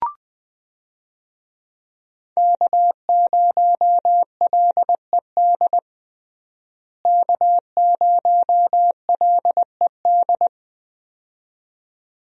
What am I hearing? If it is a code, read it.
Morse code, heard as a keyed tone, about 20 wpm, 700 Hz. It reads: K0LED K0LED
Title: USA Call Signs - 5 Characters - Repeated 2x - 20wpm